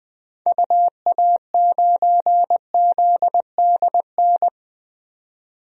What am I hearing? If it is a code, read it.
Morse code, heard as a keyed tone, about 21 wpm, 700 Hz. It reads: UA9ZDN
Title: DXCC-100 Call Signs - 6 Characters - 20wpm